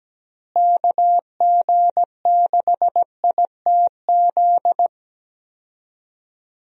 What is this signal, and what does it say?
Morse code, heard as a keyed tone, about 17 wpm, 700 Hz. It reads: KG6ITZ
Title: USA Call Signs - 6 Characters - Repeated 2x - 17wpm